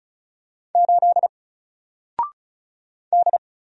Morse code, heard as a keyed tone, about 35 wpm, 700 Hz. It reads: 8 D
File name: Single-Letter-Number-35wpm